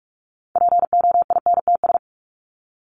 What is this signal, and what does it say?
Morse code, heard as a keyed tone, about 44 wpm, 700 Hz. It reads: POINTS
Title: ES:S Suffix - Top 100 Words with Base Words 45wpm